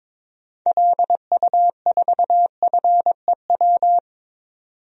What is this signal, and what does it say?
Morse code, heard as a keyed tone, about 22 wpm, 700 Hz. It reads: LU4FEW
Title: DXCC-100 Call Signs - 6 Characters - 22wpm